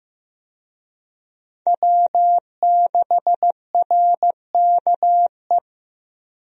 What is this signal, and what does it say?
Morse code, heard as a keyed tone, about 15 wpm, 700 Hz. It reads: W6RKE
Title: USA Call Signs - 5 Characters - Repeated 3x - 15wpm